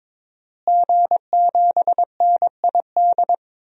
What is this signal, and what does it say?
Morse code, heard as a keyed tone, about 22 wpm, 700 Hz. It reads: G7NID
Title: DXCC-100 Call Signs - 5 Characters - Repeated 3x - 22wpm